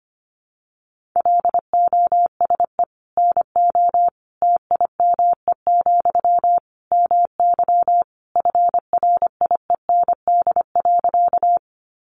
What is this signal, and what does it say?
Morse code, heard as a keyed tone, about 25 wpm, 700 Hz. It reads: LOSE NO TIME, MY FRIEND.
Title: Sentences from Top 1100 Words 25wpm